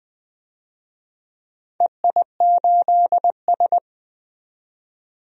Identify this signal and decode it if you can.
Morse code, heard as a keyed tone, about 20 wpm, 700 Hz. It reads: EI8S
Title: DXCC-100 Call Signs - 4 Characters - 20wpm